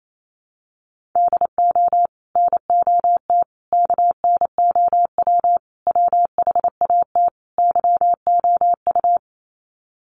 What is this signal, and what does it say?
Morse code, heard as a keyed tone, about 28 wpm, 700 Hz. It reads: DO NOT KNOW WHAT YOU
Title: Sets of 5 Words 28wpm